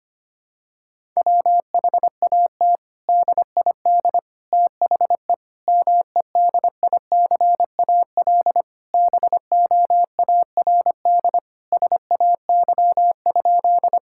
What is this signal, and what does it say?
Morse code, heard as a keyed tone, about 25 wpm, 700 Hz. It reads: WHAT DID THE MEDICAL BOARD SAY?
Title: Sentences from Top 1000 Words - Review Part 5 - 25wpm